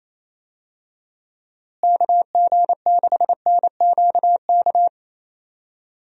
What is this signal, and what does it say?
Morse code, heard as a keyed tone, about 28 wpm, 700 Hz. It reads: KG6NQK
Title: USA Call Signs - 6 Characters - Repeated 2x - 28wpm